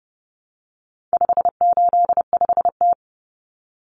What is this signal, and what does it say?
Morse code, heard as a keyed tone, about 30 wpm, 700 Hz. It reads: 585T
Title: Common Cut Numbers - 4-Digit 30wpm